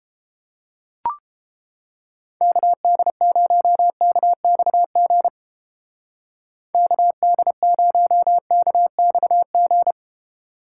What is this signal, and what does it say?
Morse code, heard as a keyed tone, about 31 wpm, 700 Hz. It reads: KD0KXG KD0KXG